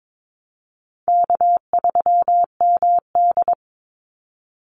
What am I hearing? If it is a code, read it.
Morse code, heard as a keyed tone, about 22 wpm, 700 Hz. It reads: K3MD
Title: Call Signs - 22wpm